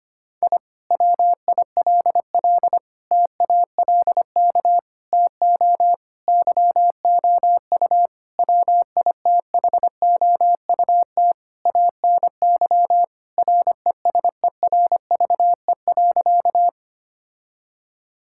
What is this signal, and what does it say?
Morse code, heard as a keyed tone, about 25 wpm, 700 Hz. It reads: I WILL TALK TO YOU WITHOUT ANY RESERVE.
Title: Sentences from Top 1000 Words - Review Part 6 - 25wpm